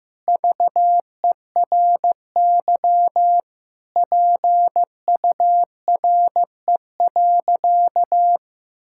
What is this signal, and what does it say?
Morse code, heard as a keyed tone, about 16 wpm, 700 Hz. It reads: VERY PURE.
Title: Sentences from Top 1600 Words 15wpm